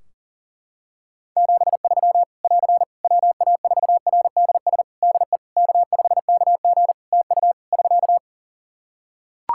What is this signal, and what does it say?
Morse code, heard as a keyed tone, about 41 wpm, 700 Hz. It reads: 73 <AR> WA4RDS DE K5KG TU <SK>
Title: Practice QSO - Exchange 4 - Ending - 40wpm